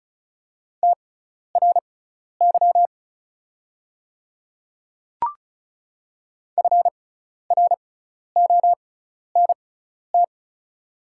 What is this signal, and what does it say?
Morse code, heard as a keyed tone, about 36 wpm, spaced as for 12 wpm, 690 Hz. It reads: TRY FRONT
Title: 6x Character Spacing - Top 500 Words - 35wpm